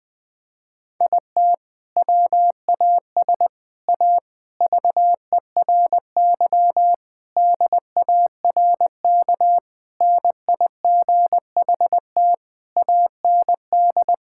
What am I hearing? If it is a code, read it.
Morse code, heard as a keyed tone, about 21 wpm, 700 Hz. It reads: IT WAS A VERY DARK NIGHT AND
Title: Sentences from Top 1600 Words 20wpm